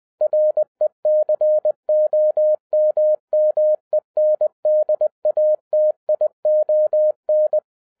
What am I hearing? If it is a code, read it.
Morse code, heard as a keyed tone, about 20 wpm, 610 Hz. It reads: RECOMMENDATION